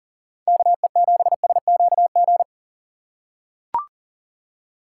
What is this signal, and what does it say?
Morse code, heard as a keyed tone, about 38 wpm, 700 Hz. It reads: KE7SQG
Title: USA Call Signs - 6 Characters - 6x Character Spacing - 40wpm